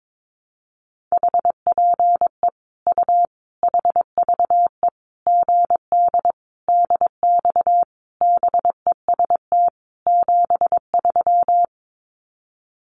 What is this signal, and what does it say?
Morse code, heard as a keyed tone, about 22 wpm, 700 Hz. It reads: HPE U HVE GD DX BEST 73
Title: Common QSO Phrases 22wpm